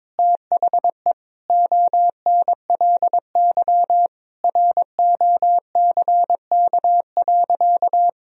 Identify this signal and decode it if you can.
Morse code, heard as a keyed tone, about 22 wpm, 700 Hz. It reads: THE ONLY ROCK.
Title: Sentences from Top 1000 Words - Review Part 2 - 22wpm